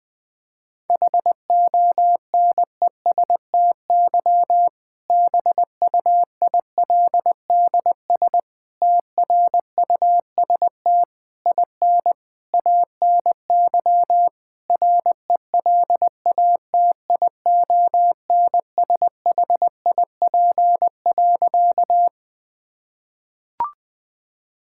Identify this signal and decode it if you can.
Morse code, heard as a keyed tone, about 20 wpm, 700 Hz. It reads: HONESTY BUILDS TRUST IN ANY RELATIONSHIP.
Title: Difficult Sentences with Hints - 20wpm